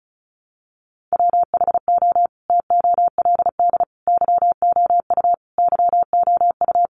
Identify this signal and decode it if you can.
Morse code, heard as a keyed tone, about 34 wpm, 700 Hz. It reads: WHO TOLD YOU YOU
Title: Sentences from Top 400 Words 35wpm